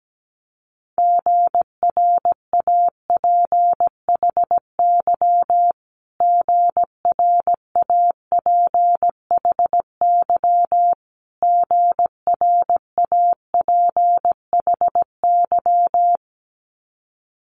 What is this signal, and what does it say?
Morse code, heard as a keyed tone, about 17 wpm, 700 Hz. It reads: GRAPHY GRAPHY GRAPHY
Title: Multi-letter Phonemes Repeated 3x 17wpm